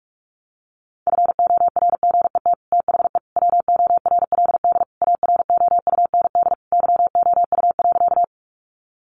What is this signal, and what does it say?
Morse code, heard as a keyed tone, about 44 wpm, 700 Hz. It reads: FORGET THE WORLD AROUND YOU.
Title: Sentences from Top 1200 Words 45wpm